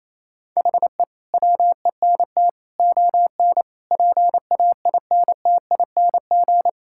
Morse code, heard as a keyed tone, about 27 wpm, 700 Hz. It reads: HE WENT ON PAINTING
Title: Sentences from Top 1000 Words - Review Part 6 - 28wpm